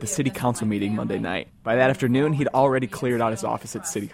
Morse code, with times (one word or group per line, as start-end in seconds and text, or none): none